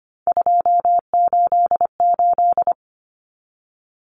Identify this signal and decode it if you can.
Morse code, heard as a keyed tone, about 25 wpm, 700 Hz. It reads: 288
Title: Common Cut Numbers - 3-Digit 25wpm